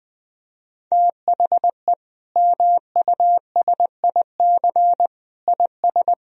Morse code, heard as a keyed tone, about 21 wpm, 700 Hz. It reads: THE MUSIC IS
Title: Sentences from Top 1000 Words - Review Part 1 - 20wpm